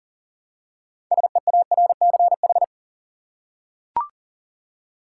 Morse code, heard as a keyed tone, about 40 wpm, 700 Hz. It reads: SEARCH
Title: Six Letter Words - 40wpm